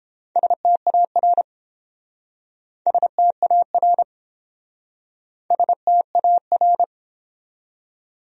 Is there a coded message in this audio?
Yes: Morse code from 0.4 to 6.8 s, heard as a keyed tone, about 29 wpm, 700 Hz. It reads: STAR STAR STAR